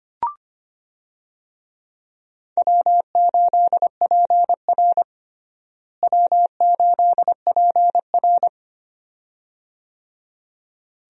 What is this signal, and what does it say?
Morse code, heard as a keyed tone, about 25 wpm, 700 Hz. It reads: W8PR W8PR
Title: USA Call Signs - 4 Characters - Repeated 2x - 25wpm